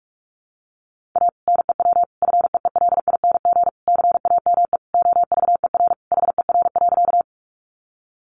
Morse code, heard as a keyed tone, about 44 wpm, 700 Hz. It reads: A NEW FEELING CAME OVER HER.
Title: Sentences from Top 1200 Words 45wpm